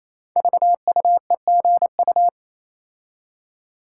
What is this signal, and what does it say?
Morse code, heard as a keyed tone, about 27 wpm, 700 Hz. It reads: VUEGU
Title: Code Groups - 5 Letters - 28wpm